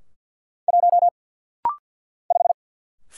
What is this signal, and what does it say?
Morse code, heard as a keyed tone, about 50 wpm, 700 Hz. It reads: 1 5
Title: Single Letter-Number Alt - Rapid-Fire 50wpm